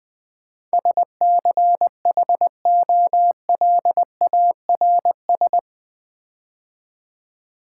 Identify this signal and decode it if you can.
Morse code, heard as a keyed tone, about 20 wpm, 700 Hz. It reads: SCHOLARS
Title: Build A Word - 4 Letters At A Time - 11-Letter Words - 20wpm